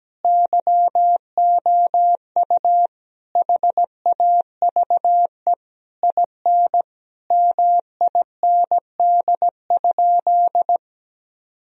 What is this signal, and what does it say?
Morse code, heard as a keyed tone, about 17 wpm, 700 Hz. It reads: YOU HAVE IN MIND?
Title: Sentences from Top 1000 Words - Review Part 3 - 17wpm